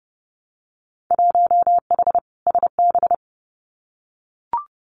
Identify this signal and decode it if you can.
Morse code, heard as a keyed tone, about 30 wpm, 700 Hz. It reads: 1H SB
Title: Winter Field Day Exchanges 30wpm